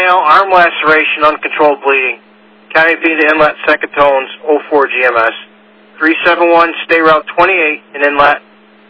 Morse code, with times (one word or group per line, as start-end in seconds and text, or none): none